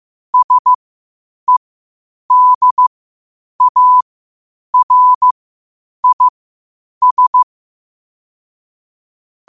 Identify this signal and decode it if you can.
Morse code, heard as a keyed tone, about 15 wpm, spaced as for 8 wpm, 1000 Hz. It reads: SEDARIS
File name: qotdfiles / 8wpm